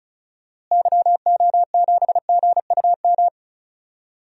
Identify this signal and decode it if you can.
Morse code, heard as a keyed tone, about 36 wpm, 700 Hz. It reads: YO7GUM
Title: DXCC-100 Call Signs - 6 Characters - Repeated 2x - 35wpm